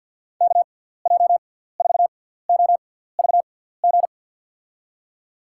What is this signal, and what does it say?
Morse code, heard as a keyed tone, about 53 wpm, spaced as for 17 wpm, 700 Hz. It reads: KJ4OVG
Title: USA Call Signs - 6 Characters - 6x Character Spacing - 50wpm